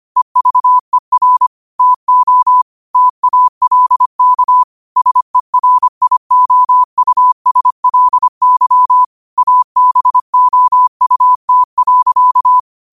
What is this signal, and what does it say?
Morse code, heard as a keyed tone, about 25 wpm, 1000 Hz. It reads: EVER TO TALK SERIOUSLY ABOUT.
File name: qotdfiles / 25wpm